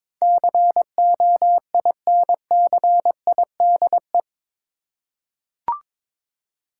Morse code, heard as a keyed tone, about 22 wpm, 700 Hz. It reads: COINCIDE
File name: practice / 22wpm / Eight Letter Words - 22wpm